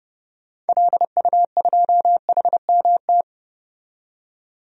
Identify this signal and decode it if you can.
Morse code, heard as a keyed tone, about 32 wpm, 700 Hz. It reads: LU2HMT
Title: DXCC-100 Call Signs - 6 Characters - 30wpm